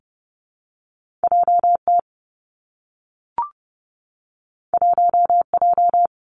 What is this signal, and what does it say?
Morse code, heard as a keyed tone, about 30 wpm, 700 Hz. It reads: J T 1 J
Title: Two Challenging Letter-Number Combinations 30wpm